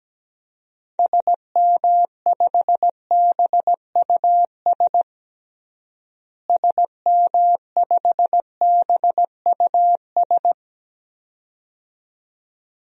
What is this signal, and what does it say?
Morse code, heard as a keyed tone, about 17 wpm, 700 Hz. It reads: SM5BUS SM5BUS
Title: DXCC-100 Call Signs - 6 Characters - Repeated 2x - 17wpm